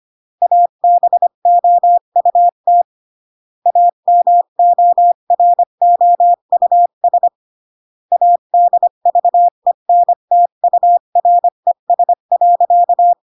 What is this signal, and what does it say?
Morse code, heard as a keyed tone, about 25 wpm, 700 Hz. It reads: ABOUT AMOROUS ADVENTURES.